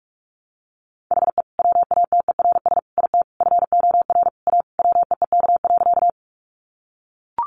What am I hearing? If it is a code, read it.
Morse code, heard as a keyed tone, about 44 wpm, 700 Hz. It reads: HE WATERS IT FOR A WEEK.
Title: Easy Phrases 45wpm